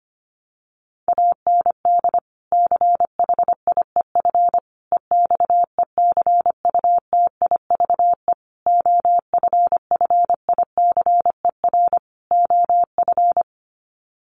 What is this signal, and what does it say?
Morse code, heard as a keyed tone, about 25 wpm, 700 Hz. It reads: AND CHIEF EXECUTIVE OFFICER OF
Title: Sets of 5 Words Encore2 25wpm